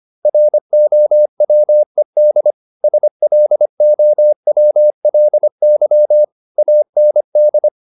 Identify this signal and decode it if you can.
Morse code, heard as a keyed tone, about 25 wpm, 600 Hz. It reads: ROWED SLOWLY AND
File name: the old man and the sea part 1 25wpm